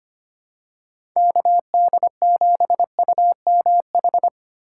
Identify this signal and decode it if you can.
Morse code, heard as a keyed tone, about 25 wpm, 700 Hz. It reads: KD7UMH
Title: USA Call Signs - 6 Characters - 4x Character Spacing - 25wpm